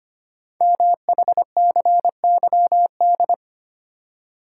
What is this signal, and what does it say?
Morse code, heard as a keyed tone, about 25 wpm, 700 Hz. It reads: MHCYD
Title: Code Groups - 5 Letters - 25wpm